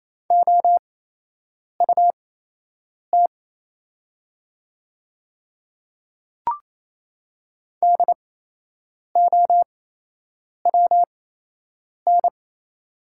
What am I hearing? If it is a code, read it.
Morse code, heard as a keyed tone, about 27 wpm, spaced as for 8 wpm, 690 Hz. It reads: OUT DOWN
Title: 8x Character Spacing - Top 300 Words - 28wpm